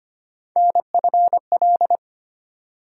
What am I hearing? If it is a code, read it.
Morse code, heard as a keyed tone, about 25 wpm, 700 Hz. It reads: NFL
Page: morse-code-ninja-practice-25wpm